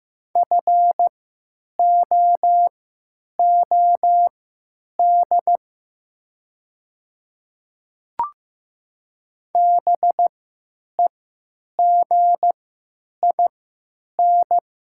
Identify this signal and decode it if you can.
Morse code, heard as a keyed tone, about 15 wpm, spaced as for 9 wpm, 690 Hz. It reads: FOOD BEGIN
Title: 3x Character Spacing - Top 300 Words - 15wpm